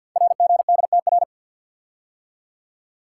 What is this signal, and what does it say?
Morse code, heard as a keyed tone, about 50 wpm, 700 Hz. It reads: RGDTR
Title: Code Groups - 5 Letters - 50wpm